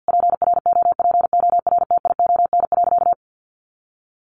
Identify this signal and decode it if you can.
Morse code, heard as a keyed tone, about 50 wpm, 700 Hz. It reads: PROPORTION.
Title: A Sound Of Thunder 50wpm